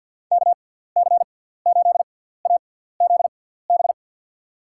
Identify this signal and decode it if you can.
Morse code, heard as a keyed tone, about 47 wpm, spaced as for 17 wpm, 700 Hz. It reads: KC8AZB
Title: USA Call Signs - 6x Character Spacing - 50wpm